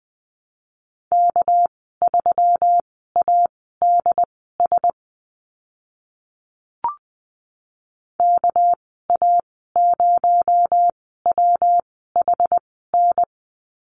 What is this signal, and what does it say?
Morse code, heard as a keyed tone, about 20 wpm, spaced as for 14 wpm, 700 Hz. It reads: K3ADS KA0WHN
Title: 219 Lesson 35g - Number 0 - Call Signs 20wpm